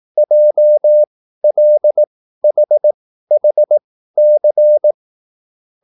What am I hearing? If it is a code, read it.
Morse code, heard as a keyed tone, about 18 wpm, spaced as for 13 wpm, 600 Hz. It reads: JLHHC